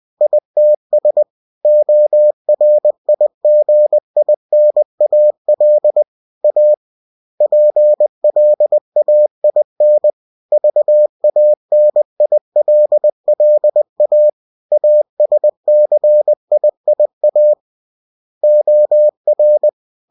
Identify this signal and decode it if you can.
Morse code, heard as a keyed tone, about 20 wpm, 600 Hz. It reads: ITS ORIGINAL A PLAIN VANILLA ASCIIA OR
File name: cw-books / winnie-the-pooh / winnie-20wpm-page-0116